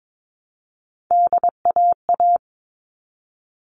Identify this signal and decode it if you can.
Morse code, heard as a keyed tone, about 22 wpm, 700 Hz. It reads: DAA
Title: All Cut Numbers - 3-Digit 22wpm